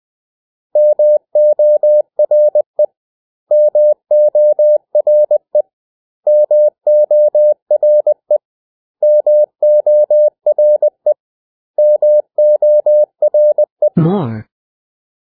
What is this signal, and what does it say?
Morse code, heard as a keyed tone, about 20 wpm, 600 Hz. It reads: MORE MORE MORE MORE MORE
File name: cw / 100 Common English Words